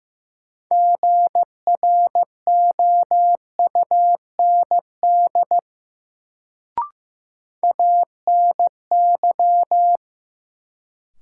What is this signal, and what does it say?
Morse code, heard as a keyed tone, about 15 wpm, 700 Hz. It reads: GROUND ANY
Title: Sets of 3 Words - New Word Drill - 15wpm